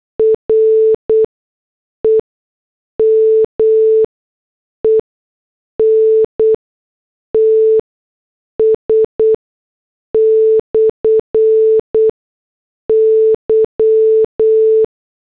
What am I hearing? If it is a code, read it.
Morse code, heard as a keyed tone, about 8 wpm, spaced as for 6 wpm, 440 Hz. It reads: REMENTS/Y
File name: morse / headlines